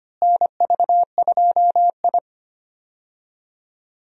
Morse code, heard as a keyed tone, about 25 wpm, 700 Hz. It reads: NV2I
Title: USA Call Signs - 4 Characters - Repeated 2x - 25wpm